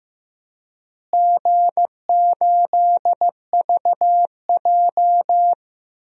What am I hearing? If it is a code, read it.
Morse code, heard as a keyed tone, about 15 wpm, 700 Hz. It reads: G8VJ
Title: DXCC-100 Call Signs - 4 Characters - Repeated 3x - 15wpm